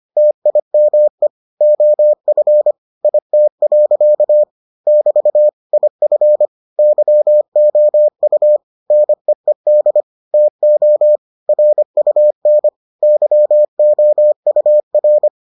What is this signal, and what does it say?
Morse code, heard as a keyed tone, about 25 wpm, 600 Hz. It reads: TIME OF IT. <BT> IF YOU NEED TO RUN YOUR